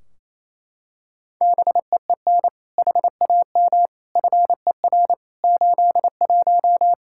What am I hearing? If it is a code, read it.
Morse code, heard as a keyed tone, about 27 wpm, 700 Hz. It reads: BEEN HAM FER 81
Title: Practice QSO - Exchange 3 - More Info - 28wpm